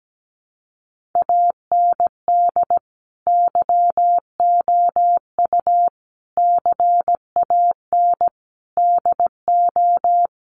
Morse code, heard as a keyed tone, about 17 wpm, 700 Hz. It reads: AND YOU CAN DO
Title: Sentences from Top 500 Words 17wpm